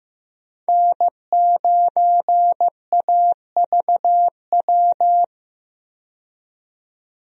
Morse code, heard as a keyed tone, about 16 wpm, 700 Hz. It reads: N9AVW
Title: USA Call Signs - 5 Characters - 15wpm